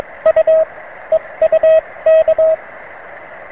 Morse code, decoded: U EU K